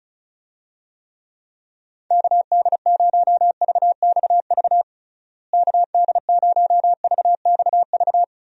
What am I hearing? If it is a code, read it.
Morse code, heard as a keyed tone, about 34 wpm, 700 Hz. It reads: KD0VXV KD0VXV
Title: USA Call Signs - 6 Characters - Repeated 3x - 35wpm